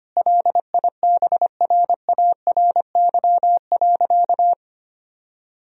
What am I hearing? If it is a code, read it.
Morse code, heard as a keyed tone, about 25 wpm, 700 Hz. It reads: LIBRARY.
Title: Sentences from Top 1000 Words - Review Part 3 - 25wpm